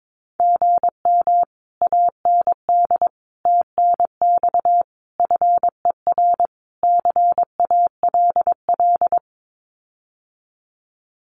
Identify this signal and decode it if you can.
Morse code, heard as a keyed tone, about 22 wpm, 700 Hz. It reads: GM AND TNX FER CALL
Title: Common QSO Phrases Encore 22wpm